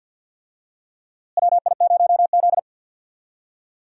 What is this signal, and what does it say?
Morse code, heard as a keyed tone, about 51 wpm, 700 Hz. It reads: WI0Z
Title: USA Call Signs - 4 Characters - Repeated 3x - 50wpm